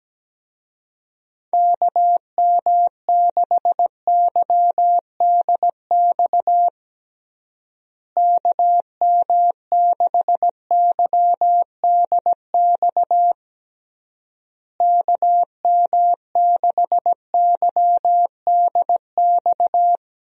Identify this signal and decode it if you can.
Morse code, heard as a keyed tone, about 17 wpm, 700 Hz. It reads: KM6YDX KM6YDX KM6YDX
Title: USA Call Signs - 6 Characters - Repeated 3x - 17wpm